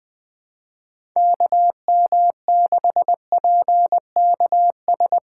Morse code, heard as a keyed tone, about 22 wpm, 700 Hz. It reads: KM6PKS
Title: Speed Racing - USA Call Signs 20wpm